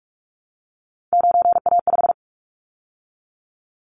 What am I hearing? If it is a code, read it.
Morse code, heard as a keyed tone, about 46 wpm, 700 Hz. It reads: 9A5
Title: Callsign Prefixes for Top 100 DXCC - 45wpm